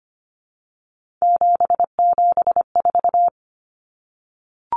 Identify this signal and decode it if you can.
Morse code, heard as a keyed tone, about 25 wpm, 700 Hz. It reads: 774
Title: Common Cut Numbers - 3-Digit 25wpm